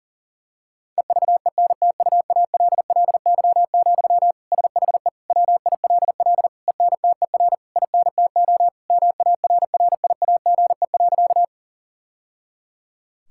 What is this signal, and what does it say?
Morse code, heard as a keyed tone, about 40 wpm, 690 Hz. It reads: EVENTUALLY, SHE WILL ENTER INTO MARRIAGE.
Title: Sentences from Top 1000 Words - Review Part 6 - 40wpm